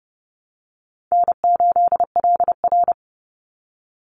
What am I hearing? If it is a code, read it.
Morse code, heard as a keyed tone, about 30 wpm, 700 Hz. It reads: N8LR
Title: Easy Call Signs - 30wpm